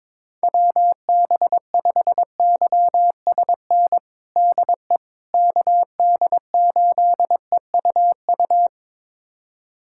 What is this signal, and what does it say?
Morse code, heard as a keyed tone, about 22 wpm, 700 Hz. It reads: WB5YSN DE KD8EUU
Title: USA Callsign DE Callsign - 6 Characters - 22wpm